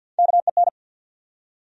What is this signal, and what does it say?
Morse code, heard as a keyed tone, about 48 wpm, 700 Hz. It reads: KEN
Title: USA Common Names - Max 3 Letters - 50wpm